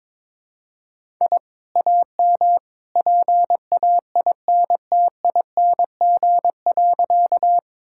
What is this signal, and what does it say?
Morse code, heard as a keyed tone, about 22 wpm, 700 Hz. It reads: I AM PAINTING.
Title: Sentences from Top 1000 Words - Review Part 6 - 22wpm